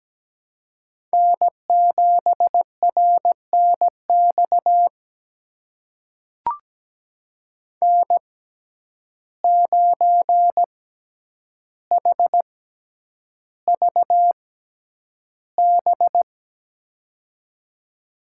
Morse code, heard as a keyed tone, about 17 wpm, spaced as for 9 wpm, 700 Hz. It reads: N7RNX N9HVB
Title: USA Call Signs - 5 Characters - 6x Character Spacing - 17wpm